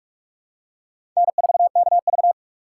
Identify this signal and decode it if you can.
Morse code, heard as a keyed tone, about 46 wpm, 700 Hz. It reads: N4KV
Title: USA Call Signs - 4 Characters - 2x Character Spacing - 45wpm